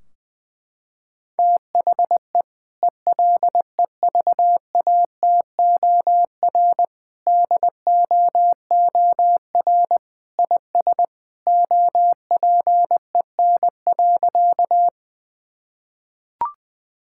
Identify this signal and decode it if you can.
Morse code, heard as a keyed tone, about 20 wpm, 700 Hz. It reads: THE ELEVATOR DOOR IS OPEN.
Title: Sentences from Top 1600 Words 20wpm